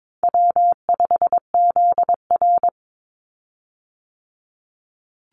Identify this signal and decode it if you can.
Morse code, heard as a keyed tone, about 22 wpm, 700 Hz. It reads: W5ZR
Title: Easy Call Signs Repeated 2x - 22wpm